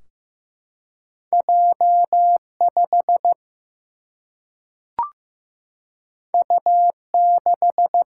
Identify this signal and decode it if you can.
Morse code, heard as a keyed tone, about 15 wpm, 700 Hz. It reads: J5 U6
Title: Single-Letter Single-Number 15wpm